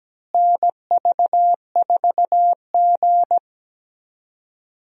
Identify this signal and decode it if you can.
Morse code, heard as a keyed tone, about 17 wpm, 700 Hz. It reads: NV4G
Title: USA Call Signs - 4 Characters - Repeated 3x - 17wpm